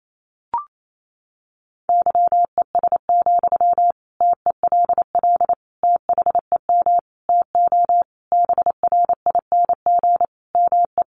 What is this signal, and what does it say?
Morse code, heard as a keyed tone, about 28 wpm, 700 Hz. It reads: YES, TELL THEM TO BRING ME
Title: Sentences from Top 500 Words 28wpm